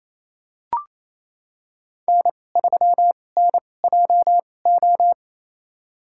Morse code, heard as a keyed tone, about 28 wpm, spaced as for 20 wpm, 700 Hz. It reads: N3NJO
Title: USA Call Signs - 5 Characters - 2x Character Spacing - 28wpm